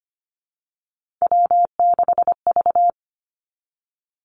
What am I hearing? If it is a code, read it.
Morse code, heard as a keyed tone, about 25 wpm, 700 Hz. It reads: W6V
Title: Three Challenging Letter-Number Combinations 25wpm